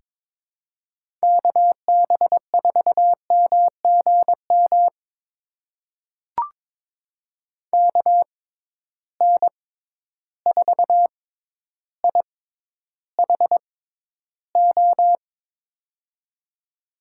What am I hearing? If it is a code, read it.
Morse code, heard as a keyed tone, about 23 wpm, spaced as for 11 wpm, 700 Hz. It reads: KB4MGM KN4IHO
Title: USA Call Signs - 6 Characters - 6x Character Spacing - 22wpm